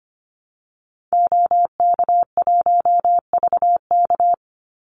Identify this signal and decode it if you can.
Morse code, heard as a keyed tone, about 25 wpm, 700 Hz. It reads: OK1VK
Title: Call Signs - 25wpm